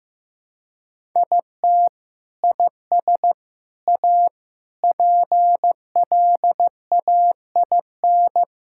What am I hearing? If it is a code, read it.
Morse code, heard as a keyed tone, about 15 wpm, 700 Hz. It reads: IT IS A PLAIN
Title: Sentences from Top 1000 Words - Review Part 5 - 15wpm